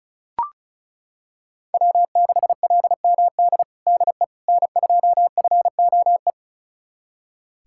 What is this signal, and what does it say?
Morse code, heard as a keyed tone, about 36 wpm, 700 Hz. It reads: W6LMD DE N2FOE
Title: USA Callsign DE Callsign - 5 Characters - 35wpm